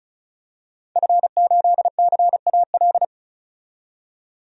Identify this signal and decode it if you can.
Morse code, heard as a keyed tone, about 34 wpm, 700 Hz. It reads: F8CAL
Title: DXCC-100 Call Signs - 5 Characters - 35wpm